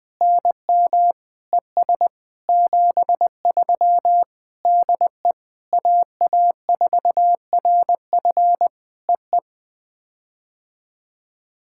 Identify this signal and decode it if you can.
Morse code, heard as a keyed tone, about 19 wpm, 700 Hz. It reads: NM ES 73 DE AA4RF EE
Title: POTA QSO Practice - 20wpm